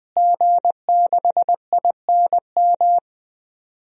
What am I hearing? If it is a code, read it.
Morse code, heard as a keyed tone, about 20 wpm, 700 Hz. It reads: G6INM
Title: DXCC-100 Call Signs - 5 Characters - Repeated 2x - 20wpm